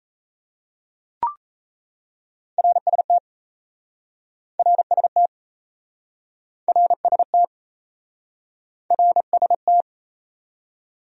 Morse code, heard as a keyed tone, about 37 wpm, 700 Hz. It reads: RST RST RST RST